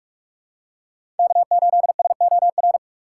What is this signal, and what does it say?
Morse code, heard as a keyed tone, about 42 wpm, 700 Hz. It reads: K8SOR
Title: USA Call Signs - 3x Character Spacing - 45wpm